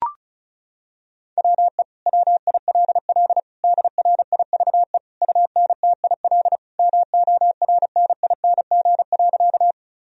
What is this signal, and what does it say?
Morse code, heard as a keyed tone, about 34 wpm, 700 Hz. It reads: WE WILL DRIVE UNTIL MORNING.
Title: Sentences from Top 1000 Words - Review Part 2 - 35wpm